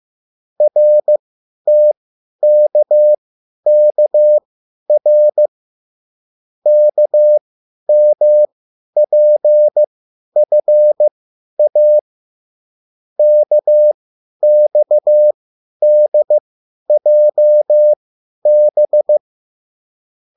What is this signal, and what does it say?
Morse code, heard as a keyed tone, about 15 wpm, spaced as for 10 wpm, 600 Hz. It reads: RTKKR KMPFA KXDJB